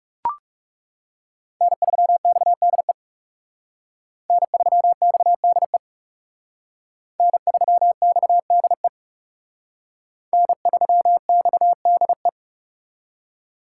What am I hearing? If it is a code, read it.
Morse code, heard as a keyed tone, about 39 wpm, 700 Hz. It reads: N3XDE N3XDE N3XDE N3XDE